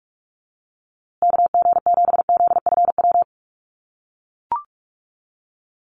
Morse code, heard as a keyed tone, about 44 wpm, 700 Hz. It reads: KG7ZFW
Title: USA Call Signs Repeated 2x 45wpm